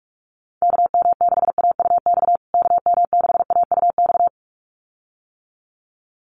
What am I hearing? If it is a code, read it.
Morse code, heard as a keyed tone, about 44 wpm, 700 Hz. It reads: KM6AUX KM6AUX
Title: USA Call Signs Repeated 2x 45wpm